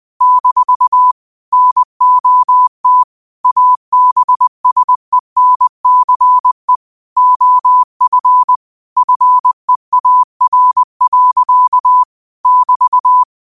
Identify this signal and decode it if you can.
Morse code, heard as a keyed tone, about 20 wpm, 1000 Hz. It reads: - NOT ABSENCE OF FEAR. <BT>
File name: qotdfiles / 20wpm